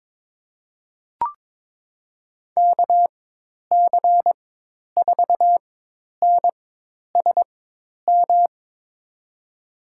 Morse code, heard as a keyed tone, about 22 wpm, spaced as for 10 wpm, 700 Hz. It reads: KC4NSM
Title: USA Call Signs - 6 Characters - 4x Character Spacing - 22wpm